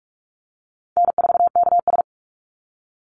Morse code, heard as a keyed tone, about 44 wpm, 700 Hz. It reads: N4KS
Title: Easy Call Signs - 45wpm